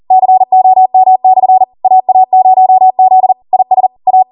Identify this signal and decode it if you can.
Morse code, heard as a keyed tone, about 40 wpm, 750 Hz. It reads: COM/ AA0Z IS A